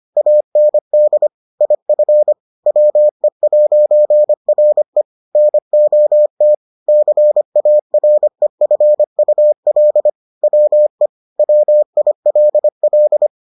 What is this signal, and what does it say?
Morse code, heard as a keyed tone, about 25 wpm, 600 Hz. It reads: AND IF WE'RE NOT CAREFUL WE WILL